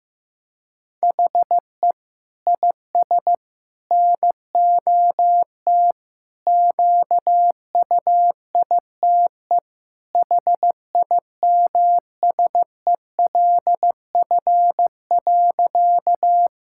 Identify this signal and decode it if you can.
Morse code, heard as a keyed tone, about 15 wpm, 700 Hz. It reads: HE IS NOT QUITE HIMSELF.
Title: Sentences from Top 1600 Words 15wpm